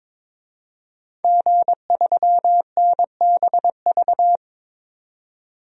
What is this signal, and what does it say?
Morse code, heard as a keyed tone, about 22 wpm, 700 Hz. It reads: G3NBV
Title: DXCC-100 Call Signs - 5 Characters - 22wpm